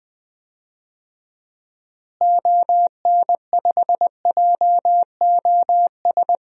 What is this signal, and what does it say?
Morse code, heard as a keyed tone, about 20 wpm, 700 Hz. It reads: ON5JOS
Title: DXCC-100 Call Signs - 6 Characters - Repeated 2x - 20wpm